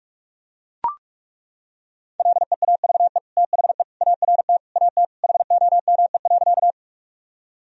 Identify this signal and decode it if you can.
Morse code, heard as a keyed tone, about 41 wpm, 700 Hz. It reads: LEAVE THE ART AT HOME.
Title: Sentences from Top 1000 Words - Review Part 3 - 45wpm